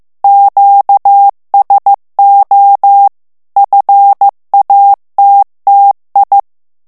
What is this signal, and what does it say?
Morse code, heard as a keyed tone, about 15 wpm, 800 Hz. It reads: QSO FATTI
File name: morse mp3 / ITALIEN texte clair 2 vitesse 14 mots minute